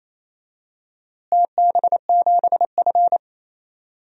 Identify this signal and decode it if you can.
Morse code, heard as a keyed tone, about 28 wpm, 700 Hz. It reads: TB7F
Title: DXCC-100 Call Signs - 4 Characters - Repeated 2x - 28wpm